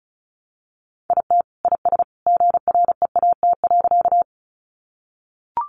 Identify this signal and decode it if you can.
Morse code, heard as a keyed tone, about 34 wpm, 700 Hz. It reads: IT IS GREAT.
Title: Sentences from Top 200 Words 35wpm